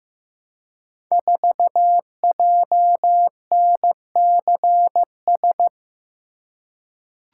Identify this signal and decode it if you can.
Morse code, heard as a keyed tone, about 15 wpm, 700 Hz. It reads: 4JNCS
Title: Code Groups - 5 Letter:Numbers - 15wpm